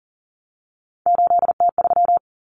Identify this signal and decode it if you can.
Morse code, heard as a keyed tone, about 40 wpm, 700 Hz. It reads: 8T3
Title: Common Cut Numbers - 3-Digit 40wpm